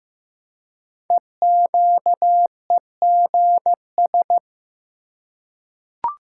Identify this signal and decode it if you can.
Morse code, heard as a keyed tone, about 15 wpm, 700 Hz. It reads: EQEGS
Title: Code Groups - 5 Letters - 15wpm